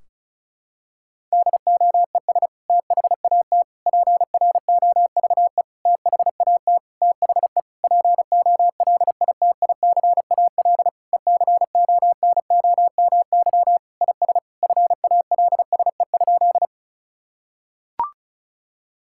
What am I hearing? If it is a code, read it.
Morse code, heard as a keyed tone, about 36 wpm, 700 Hz. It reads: DOES THAT PROVE THAT THE POLITICAL ECONOMY IS FALSE?
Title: Sentences from Top 1600 Words 35wpm